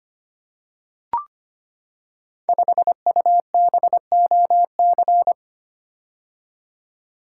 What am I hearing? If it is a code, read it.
Morse code, heard as a keyed tone, about 25 wpm, 700 Hz. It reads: 5UBOC
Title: Code Groups - 5 Letter:Numbers - 25wpm